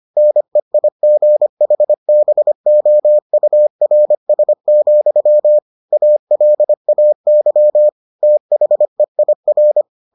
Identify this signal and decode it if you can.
Morse code, heard as a keyed tone, about 25 wpm, 600 Hz. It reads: NEIGHBOURS, ALAY THEIR